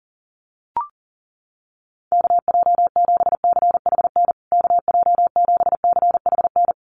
Morse code, heard as a keyed tone, about 40 wpm, 700 Hz. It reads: KJ7CHN KJ7CHN
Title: USA Call Signs Repeated 3x 40wpm